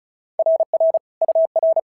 Morse code, heard as a keyed tone, about 34 wpm, 650 Hz. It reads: RR UR
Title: SOTA QSO Practice 35wpm